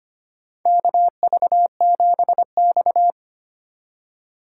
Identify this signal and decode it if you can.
Morse code, heard as a keyed tone, about 25 wpm, 700 Hz. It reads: KV7X
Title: USA Call Signs - 4 Characters - 4x Character Spacing - 25wpm